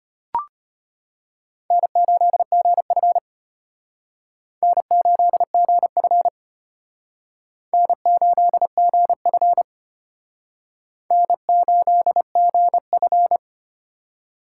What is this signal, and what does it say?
Morse code, heard as a keyed tone, about 31 wpm, 700 Hz. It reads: N8GF N8GF N8GF N8GF
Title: Speed Racing - USA 1x2 Call Signs 25wpm